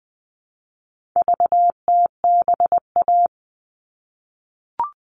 Morse code, heard as a keyed tone, about 20 wpm, 700 Hz. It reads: VTBA
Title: All Cut Numbers - 4-Digit 20wpm